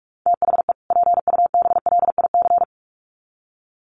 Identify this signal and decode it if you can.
Morse code, heard as a keyed tone, about 44 wpm, 700 Hz. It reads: THE PUBLIC
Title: Sets of 2 Words 45wpm